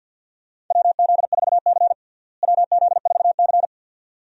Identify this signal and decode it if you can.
Morse code, heard as a keyed tone, about 47 wpm, 700 Hz. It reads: WZ4C WZ4C
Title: USA Call Signs - 4 Characters - Repeated 2x - 50wpm